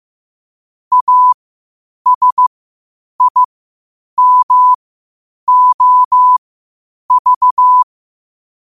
Morse code, heard as a keyed tone, about 15 wpm, spaced as for 8 wpm, 1000 Hz. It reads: ASIMOV
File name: qotdfiles / 8wpm